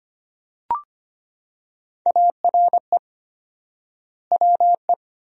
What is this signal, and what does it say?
Morse code, heard as a keyed tone, about 25 wpm, 700 Hz. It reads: ARE WE
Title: 4x Word-Spacing - Sentences from Top 400 Words - 25wpm